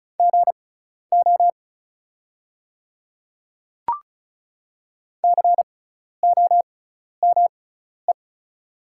Morse code, heard as a keyed tone, about 36 wpm, spaced as for 12 wpm, 690 Hz. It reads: GO COME
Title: 6x Character Spacing - Top 100 Words - 35wpm